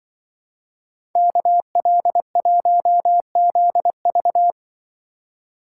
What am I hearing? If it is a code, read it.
Morse code, heard as a keyed tone, about 25 wpm, 700 Hz. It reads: KL1ZV
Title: Speed Racing - USA 2x2 Call Signs 20wpm